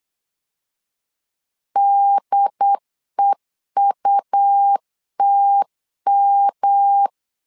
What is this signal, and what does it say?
Morse code, heard as a keyed tone, about 8 wpm, 800 Hz. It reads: DEUTM
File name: LEZIONI / 3 MWRO / 3 MW 8wpm 2501